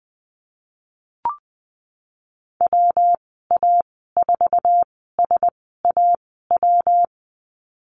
Morse code, heard as a keyed tone, about 20 wpm, spaced as for 14 wpm, 700 Hz. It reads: WA4SAW
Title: 147 Lesson 24g - Letter y - Call Signs 20wpm